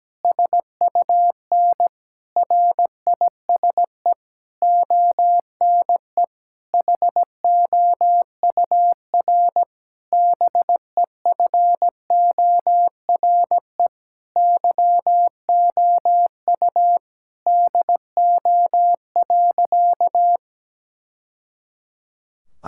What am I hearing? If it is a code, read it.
Morse code, heard as a keyed tone, about 17 wpm, 700 Hz. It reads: SUN RISE ONE HOUR BEFORE YOU DO.
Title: Sentences from Top 1000 Words - Review Part 6 - 17wpm